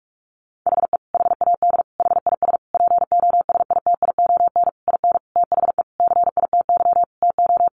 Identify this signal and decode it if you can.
Morse code, heard as a keyed tone, about 44 wpm, 700 Hz. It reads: HE HAD HIS POSITION IN THE CITY TO
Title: Sentences from Top 900 Words 45wpm